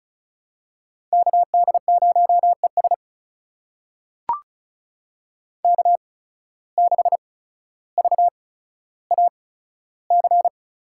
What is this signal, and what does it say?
Morse code, heard as a keyed tone, about 36 wpm, spaced as for 15 wpm, 700 Hz. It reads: KD0ES K6VAC
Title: USA Call Signs - 5 Characters - 8x Character Spacing - 35wpm